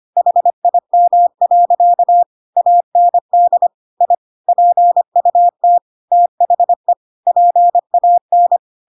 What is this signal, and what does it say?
Morse code, heard as a keyed tone, about 25 wpm, 700 Hz. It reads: HIM. AND I PUT THE PAN
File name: Pinocchio / Pinocchio ch7 25 25 ik1hge